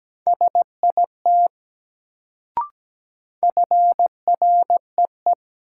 Morse code, heard as a keyed tone, about 17 wpm, 700 Hz. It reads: SIT FREE
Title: Sentences from Top 500 Words - New Word Drill - 17wpm